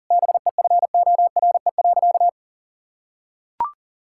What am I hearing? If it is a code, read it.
Morse code, heard as a keyed tone, about 40 wpm, 700 Hz. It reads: BEFORE.
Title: Sentences from Top 1000 Words - Review Part 5 - 40wpm